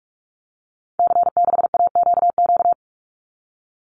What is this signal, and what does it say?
Morse code, heard as a keyed tone, about 44 wpm, 700 Hz. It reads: C6AQQ
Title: Call Signs Repeated 2x - 45wpm